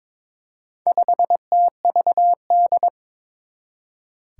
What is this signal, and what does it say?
Morse code, heard as a keyed tone, about 22 wpm, 700 Hz. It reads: 5TVD
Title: Four Challenging Letter-Number Combinations 22wpm